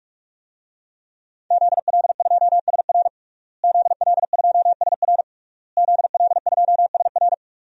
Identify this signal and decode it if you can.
Morse code, heard as a keyed tone, about 48 wpm, 700 Hz. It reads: ZL2SR ZL2SR ZL2SR
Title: DXCC-100 Call Signs - 5 Characters - Repeated 3x - 45wpm